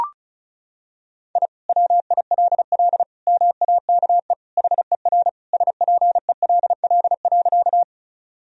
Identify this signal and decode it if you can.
Morse code, heard as a keyed tone, about 34 wpm, 700 Hz. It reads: I WILL MAKE HER SPELL.
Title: Sentences from Top 1000 Words - Review Part 1 - 35wpm